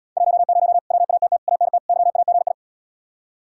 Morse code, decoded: 10 WATT ANT YAGI

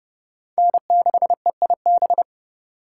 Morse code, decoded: N6EIB

